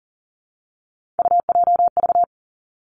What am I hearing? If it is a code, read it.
Morse code, heard as a keyed tone, about 38 wpm, 700 Hz. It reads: UJV
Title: Three Challenging Letter-Number Combinations 40wpm